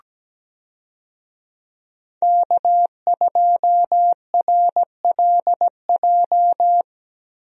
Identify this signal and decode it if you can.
Morse code, heard as a keyed tone, about 17 wpm, 700 Hz. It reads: K2RLJ
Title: USA Call Signs - 5 Characters - Repeated 2x - 17wpm